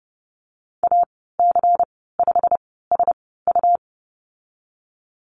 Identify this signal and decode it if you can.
Morse code, heard as a keyed tone, about 30 wpm, spaced as for 17 wpm, 700 Hz. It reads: AC5SU
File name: Lesson 19g - Letter w - Call Signs 30wpm